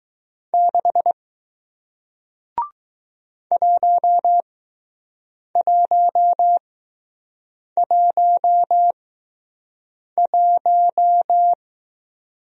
Morse code, heard as a keyed tone, about 20 wpm, spaced as for 6 wpm, 700 Hz. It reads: 6 1111